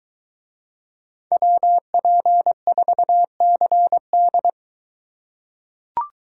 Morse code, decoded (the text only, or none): WP4CD